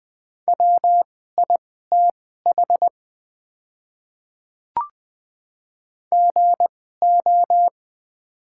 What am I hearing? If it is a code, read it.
Morse code, heard as a keyed tone, about 20 wpm, spaced as for 14 wpm, 700 Hz. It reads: WITH GO
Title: 2x Character Spacing - Top 100 Words - 20wpm